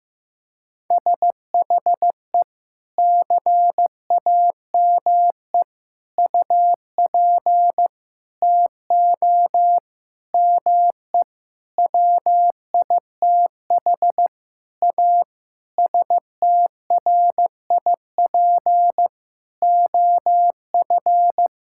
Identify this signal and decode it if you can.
Morse code, heard as a keyed tone, about 15 wpm, 700 Hz. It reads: SHE CAME UP TO ME WITH A STRIP OF
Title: Sentences from Top 1600 Words 15wpm